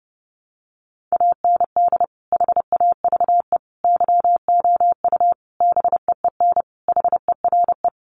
Morse code, heard as a keyed tone, about 30 wpm, 700 Hz. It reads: AND HAVE YOU BEEN HERE
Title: Sentences from Top 200 Words 30wpm